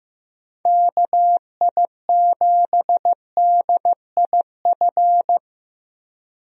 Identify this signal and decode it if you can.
Morse code, heard as a keyed tone, about 14 wpm, 700 Hz. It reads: KI7DIF
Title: USA Call Signs - 6 Characters - Repeated 3x - 15wpm